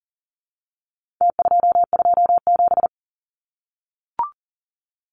Morse code, heard as a keyed tone, about 40 wpm, 700 Hz. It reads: T227
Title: Numbers - Contest Style - 4-Digit 40wpm